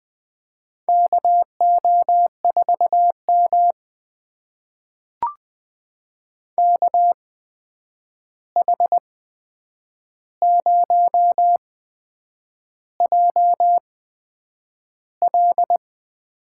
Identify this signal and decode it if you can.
Morse code, heard as a keyed tone, about 20 wpm, spaced as for 8 wpm, 700 Hz. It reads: KO4M KH0JL
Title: USA Call Signs - 8x Character Spacing - 20wpm